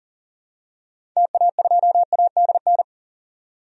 Morse code, heard as a keyed tone, about 40 wpm, 700 Hz. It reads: TA2ADN
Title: DXCC-100 Call Signs - 6 Characters - 40wpm